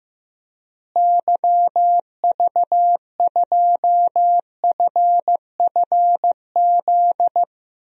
Morse code, heard as a keyed tone, about 15 wpm, 700 Hz. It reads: YV2FFZ
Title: DXCC-100 Call Signs - 6 Characters - Repeated 3x - 15wpm